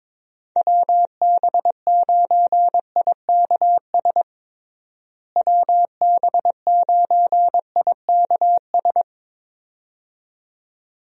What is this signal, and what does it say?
Morse code, heard as a keyed tone, about 21 wpm, 700 Hz. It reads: WB9IKS WB9IKS
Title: USA Call Signs - 6 Characters - Repeated 2x - 22wpm